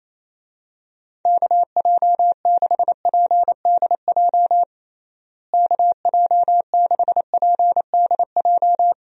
Morse code, heard as a keyed tone, about 27 wpm, 700 Hz. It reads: KJ6PDJ KJ6PDJ
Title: USA Call Signs - 6 Characters - Repeated 2x - 28wpm